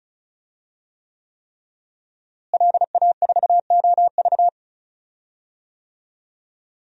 Morse code, heard as a keyed tone, about 36 wpm, 700 Hz. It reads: LA4OV
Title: DXCC-100 Call Signs - 5 Characters - Repeated 3x - 35wpm